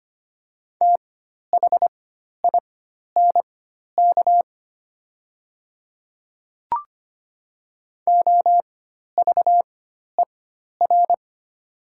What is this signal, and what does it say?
Morse code, heard as a keyed tone, about 25 wpm, spaced as for 12 wpm, 700 Hz. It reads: THINK OVER
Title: 4x Character Spacing - Top 200 Words - 25wpm